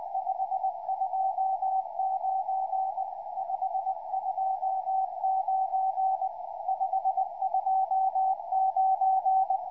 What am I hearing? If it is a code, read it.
Morse code, heard as a keyed tone, about 20 wpm, 760 Hz, with radio noise. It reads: 529 529 529
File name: pa0v 2m